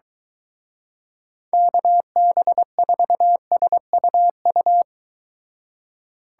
Morse code, heard as a keyed tone, about 23 wpm, 700 Hz. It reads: KB4SUU